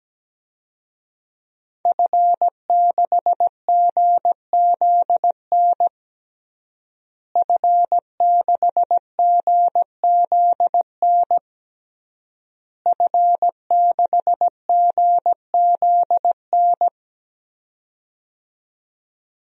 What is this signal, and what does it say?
Morse code, heard as a keyed tone, about 17 wpm, 700 Hz. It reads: F6GZN F6GZN F6GZN
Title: DXCC-100 Call Signs - 5 Characters - Repeated 3x - 17wpm